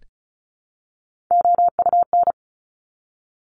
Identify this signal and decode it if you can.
Morse code, heard as a keyed tone, about 36 wpm, 700 Hz. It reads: OUN
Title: Top Three Letter Combinations in Words 35wpm